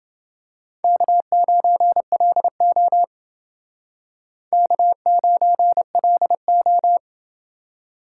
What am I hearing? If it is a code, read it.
Morse code, heard as a keyed tone, about 29 wpm, 700 Hz. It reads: K9LO K9LO